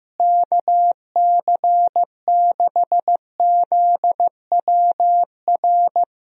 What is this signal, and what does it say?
Morse code, heard as a keyed tone, about 15 wpm, 700 Hz. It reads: KC6ZWR
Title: USA Call Signs - 6 Characters - Repeated 3x - 15wpm